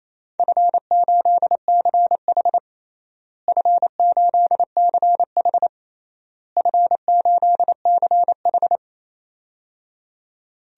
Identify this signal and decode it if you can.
Morse code, heard as a keyed tone, about 28 wpm, 700 Hz. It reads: F8CH F8CH F8CH
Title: DXCC-100 Call Signs - 4 Characters - Repeated 3x - 28wpm